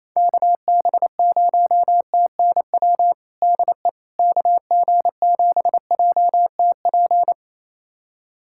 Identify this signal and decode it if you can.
Morse code, heard as a keyed tone, about 28 wpm, 700 Hz. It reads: KB0TNW DE KG7JTP
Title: USA Callsign DE Callsign - 6 Characters - 28wpm